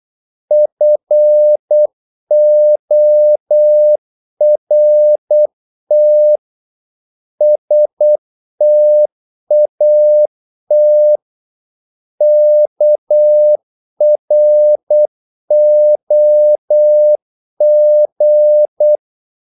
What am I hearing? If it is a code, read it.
Morse code, heard as a keyed tone, about 8 wpm, 600 Hz. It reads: FORT STAT KROG
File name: Korte ord DK 8wpm